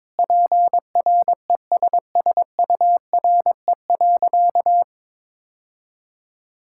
Morse code, heard as a keyed tone, about 22 wpm, 700 Hz. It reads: PRESSURE.
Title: Sentences from Top 1000 Words - Review Part 6 - 22wpm